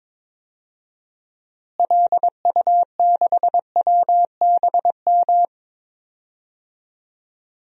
Morse code, heard as a keyed tone, about 22 wpm, 700 Hz. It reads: LU6WBM
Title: DXCC-100 Call Signs - 6 Characters - Repeated 2x - 22wpm